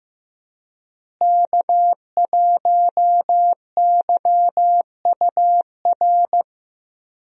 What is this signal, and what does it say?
Morse code, heard as a keyed tone, about 15 wpm, 700 Hz. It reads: K1YUR
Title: USA Call Signs - 5 Characters - Repeated 3x - 15wpm